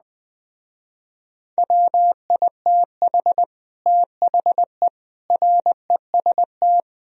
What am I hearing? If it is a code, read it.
Morse code, heard as a keyed tone, about 20 wpm, 700 Hz. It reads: WITH THE REST
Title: Speed-Racing - Sets of 5 Words 17wpm